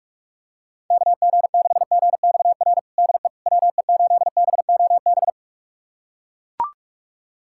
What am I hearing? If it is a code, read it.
Morse code, heard as a keyed tone, about 41 wpm, 700 Hz. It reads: KG6GXR DE WE8BOB
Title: USA Callsign DE Callsign 45wpm